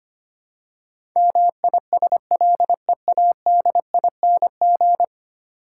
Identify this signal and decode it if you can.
Morse code, heard as a keyed tone, about 25 wpm, 700 Hz. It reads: MISLEADING
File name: practice / 25wpm / Ten Letter Words - 25wpm